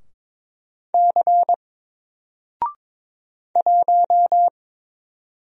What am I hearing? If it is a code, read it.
Morse code, heard as a keyed tone, about 22 wpm, 700 Hz. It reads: C 1
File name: Single-Letter-Number-22wpm